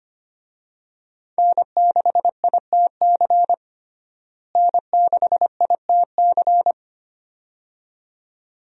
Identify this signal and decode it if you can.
Morse code, heard as a keyed tone, about 25 wpm, 700 Hz. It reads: N6ITC N6ITC
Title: USA Call Signs - 5 Characters - Repeated 2x - 25wpm